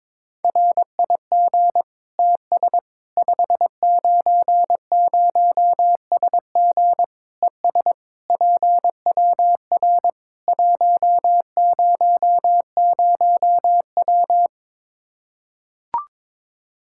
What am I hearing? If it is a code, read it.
Morse code, heard as a keyed tone, about 22 wpm, 700 Hz. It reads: RIG TS 590SG ES PWR 100W
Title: Practice QSO - Exchange 2 - Info - 22wpm